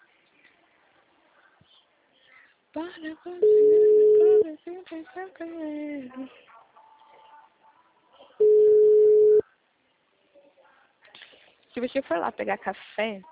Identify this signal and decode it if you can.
Ringback tone (Europe, 425 Hz), with speech.